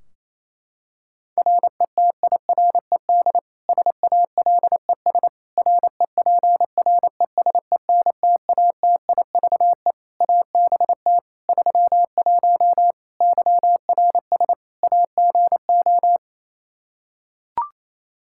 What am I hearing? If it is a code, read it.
Morse code, heard as a keyed tone, about 28 wpm, 700 Hz. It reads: RETIRED SALES REPRESENTATIVE ABT 31 YRS AGO
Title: Practice QSO - Exchange 3 - More Info - 28wpm